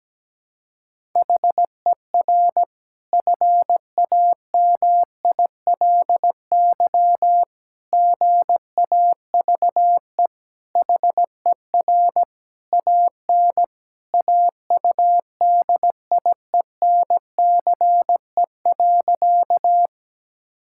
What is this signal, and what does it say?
Morse code, heard as a keyed tone, about 17 wpm, 700 Hz. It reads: HER FAMILY GAVE HER AN AUDIENCE.
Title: Sentences from Top 1000 Words - Review Part 5 - 17wpm